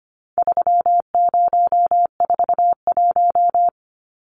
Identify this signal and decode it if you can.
Morse code, heard as a keyed tone, about 25 wpm, 700 Hz. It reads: 3041
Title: Numbers - 4-Digit 25wpm